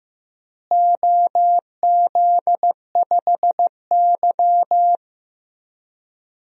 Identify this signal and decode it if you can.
Morse code, heard as a keyed tone, about 15 wpm, 700 Hz. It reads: OZ5Y
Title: DXCC-100 Call Signs - 4 Characters - Repeated 2x - 15wpm